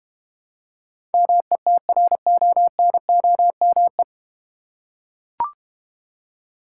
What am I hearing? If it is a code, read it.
Morse code, heard as a keyed tone, about 32 wpm, 700 Hz. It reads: METRONOME